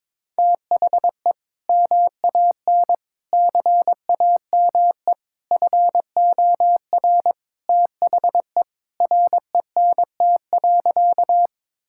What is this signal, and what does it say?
Morse code, heard as a keyed tone, about 22 wpm, 700 Hz. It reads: THE MAN CAME FOR THE RENT.
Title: Sentences from Top 1000 Words - Review Part 5 - 22wpm